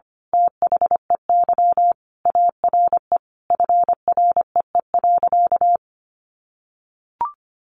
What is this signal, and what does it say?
Morse code, heard as a keyed tone, about 25 wpm, 700 Hz. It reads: THEY ARE FREE.
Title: Sentences from Top 1400 Words 25wpm